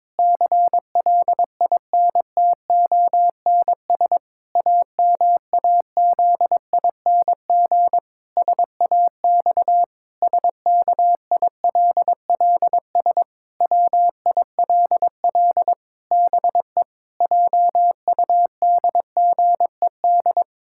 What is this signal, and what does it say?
Morse code, heard as a keyed tone, about 22 wpm, 700 Hz. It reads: CLINTONS AMAZING SAX SKILLS WILL BE JUDGED
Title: Pangram Challenge v2 - 22wpm